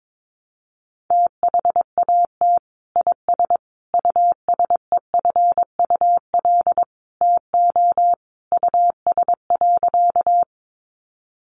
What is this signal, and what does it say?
Morse code, heard as a keyed tone, about 22 wpm, 700 Hz. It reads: THAT IS USEFUL TO US.
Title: Sentences from Top 700 Words 22wpm